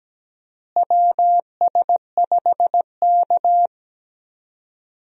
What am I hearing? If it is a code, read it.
Morse code, heard as a keyed tone, about 17 wpm, 700 Hz. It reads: WS5K
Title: USA Call Signs - 4 Characters - Repeated 2x - 17wpm